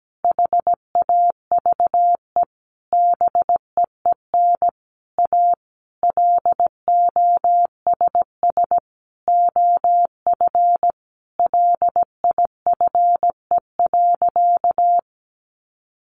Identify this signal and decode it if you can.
Morse code, heard as a keyed tone, about 17 wpm, 700 Hz. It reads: HAVE BEEN A LOSS OF LIFE.
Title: Sentences from Top 1400 Words 17wpm